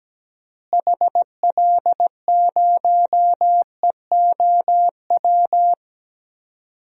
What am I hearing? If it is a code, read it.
Morse code, heard as a keyed tone, about 17 wpm, 700 Hz. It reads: HL0EOW
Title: DXCC-100 Call Signs - 6 Characters - Repeated 2x - 17wpm